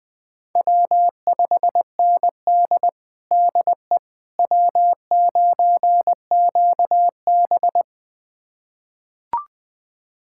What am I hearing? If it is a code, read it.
Morse code, heard as a keyed tone, about 20 wpm, 700 Hz. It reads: W5ND DE W9QB
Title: USA Callsign DE Callsign - 4 Characters - 20wpm